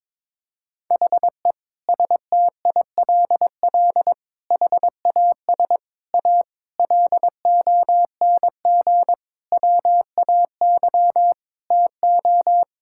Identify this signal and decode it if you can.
Morse code, heard as a keyed tone, about 22 wpm, 700 Hz. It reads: HE STILL HAS A LONG WAY TO
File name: Sentences with 7-Word Phrases - 22wpm